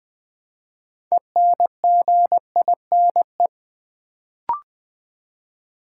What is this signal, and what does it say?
Morse code, heard as a keyed tone, about 20 wpm, 700 Hz. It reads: ENGINE
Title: Build A Word - 3 Letters At A Time - 6-Letter Words - 20wpm